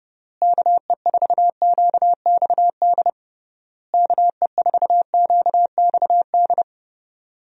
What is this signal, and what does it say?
Morse code, heard as a keyed tone, about 32 wpm, 700 Hz. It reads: KE4QXD KE4QXD
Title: USA Call Signs - 6 Characters - Repeated 3x - 30wpm